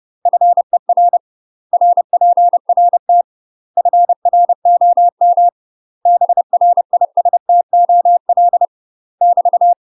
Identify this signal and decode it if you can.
Morse code, heard as a keyed tone, about 30 wpm, 700 Hz. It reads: FER RPRT FROM BRISTOL <BT>
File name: UK MASTER QSOs TRAINING / 30WPM / UK MASTER QSOs 0015